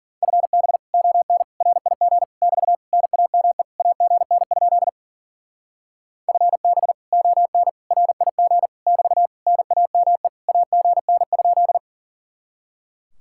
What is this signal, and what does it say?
Morse code, heard as a keyed tone, about 50 wpm, 700 Hz. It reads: FB ON RIG <BT> NAME AGN? FB ON RIG <BT> NAME AGN?